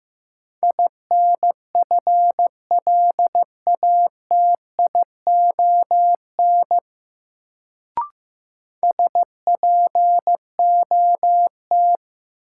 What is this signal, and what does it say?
Morse code, heard as a keyed tone, about 15 wpm, 700 Hz. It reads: INFLATION SPOT
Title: Sentences from Top 1100 Words - New Word Drill - 15wpm